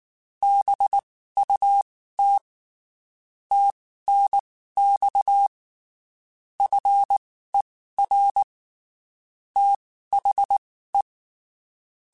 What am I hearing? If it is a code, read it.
Morse code, heard as a keyed tone, about 19 wpm, spaced as for 14 wpm, 800 Hz. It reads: BUT TNX FER THE
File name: (spaced) / QSOs Spaced-mp3s / kc7haa @18 sp